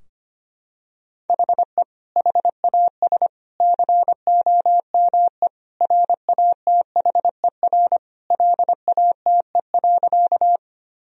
Morse code, heard as a keyed tone, about 25 wpm, 700 Hz. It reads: HE HAS COME RATHER LATE.
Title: Sentences from Top 1000 Words - Review Part 4 - 25wpm